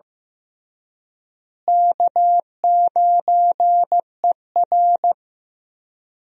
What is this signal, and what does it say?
Morse code, heard as a keyed tone, about 15 wpm, 700 Hz. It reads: K9ER
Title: USA Call Signs - 4 Characters - Repeated 3x - 15wpm